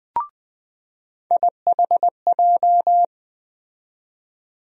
Morse code, heard as a keyed tone, about 20 wpm, 700 Hz. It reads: IHJ
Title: USA Call Sign Postfixes - 3 Letters 20wpm